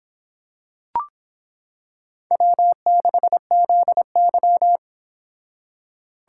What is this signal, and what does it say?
Morse code, heard as a keyed tone, about 26 wpm, 700 Hz. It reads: W6ZY